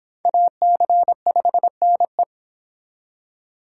Morse code, heard as a keyed tone, about 23 wpm, 700 Hz. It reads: AC5NE